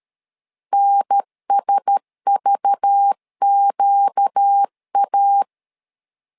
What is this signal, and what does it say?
Morse code, heard as a keyed tone, about 13 wpm, 800 Hz. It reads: NSVQA
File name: LEZIONI / 5 LBVC / 5 LBV 12wpm 1855